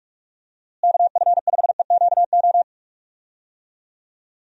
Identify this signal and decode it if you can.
Morse code, heard as a keyed tone, about 44 wpm, 700 Hz. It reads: KF5EQO
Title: USA Call Signs - 6 Characters - Repeated 2x - 45wpm